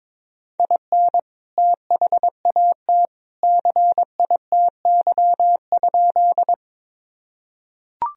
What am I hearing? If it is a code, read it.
Morse code, heard as a keyed tone, about 22 wpm, 700 Hz. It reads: IN THAT CITY?
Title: Sentences from Top 1000 Words - Review Part 1 - 22wpm